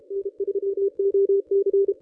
Morse code, heard as a keyed tone, about 31 wpm, 410 Hz. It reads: N3OC
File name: contests / ss-cw-k5tr-op-n5rz -1100